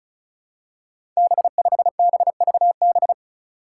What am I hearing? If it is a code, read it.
Morse code, heard as a keyed tone, about 36 wpm, 700 Hz. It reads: B5BVB
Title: Five Challenging Letter-Number Combinations 35wpm